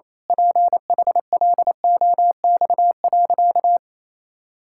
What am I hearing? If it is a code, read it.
Morse code, heard as a keyed tone, about 28 wpm, 700 Hz. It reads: PHLOX.
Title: Pangram Challenge v2 - 28wpm